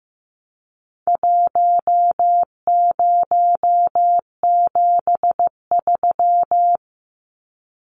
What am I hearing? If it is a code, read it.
Morse code, heard as a keyed tone, about 15 wpm, 700 Hz. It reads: 1073
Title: Numbers - 4-Digit 15wpm